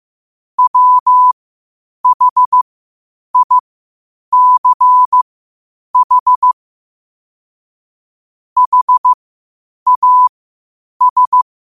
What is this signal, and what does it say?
Morse code, heard as a keyed tone, about 15 wpm, spaced as for 8 wpm, 1000 Hz. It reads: WHICH HAS
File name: qotdfiles / 8wpm